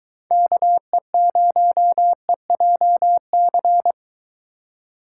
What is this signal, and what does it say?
Morse code, heard as a keyed tone, about 22 wpm, 700 Hz. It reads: KE0EJC